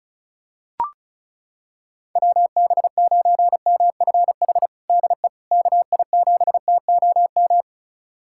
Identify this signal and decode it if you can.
Morse code, heard as a keyed tone, about 36 wpm, 690 Hz. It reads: WB9MFH DE KI7TOM
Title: USA Callsign DE Callsign - 6 Characters - 35wpm